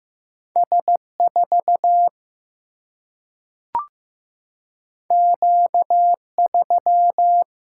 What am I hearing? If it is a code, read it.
Morse code, heard as a keyed tone, about 15 wpm, 700 Hz. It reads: S4 Q3
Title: Single-Letter Single-Number 15wpm